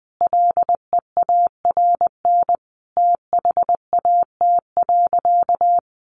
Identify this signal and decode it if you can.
Morse code, heard as a keyed tone, about 20 wpm, 700 Hz. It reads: LEARN THAT.
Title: Sentences from Top 1200 Words 20wpm